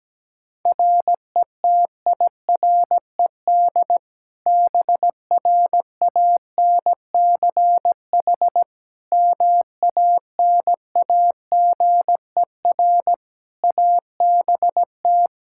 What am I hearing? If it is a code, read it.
Morse code, heard as a keyed tone, about 17 wpm, 700 Hz. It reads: RETIRED BRANCH MANAGER ABT
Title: Practice QSO - Exchange 3 - More Info - 17wpm